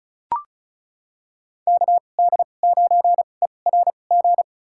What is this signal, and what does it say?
Morse code, heard as a keyed tone, about 34 wpm, spaced as for 25 wpm, 700 Hz. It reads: KD9ERG
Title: USA Call Signs - 6 Characters - 2x Character Spacing - 35wpm